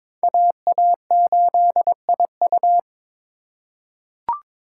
Morse code, heard as a keyed tone, about 22 wpm, 700 Hz. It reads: AA8IU
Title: USA Call Signs - 5 Characters - 4x Character Spacing - 22wpm